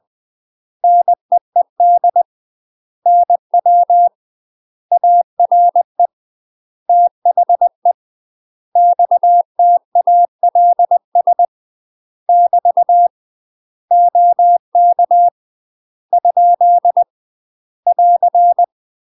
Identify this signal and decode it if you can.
Morse code, heard as a keyed tone, about 20 wpm, 700 Hz. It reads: NEED NW ARE THE XTALS <BT> OK ? <AR>
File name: UK MASTER QSOs TRAINING / 20WPM / UK MASTER QSOs 0070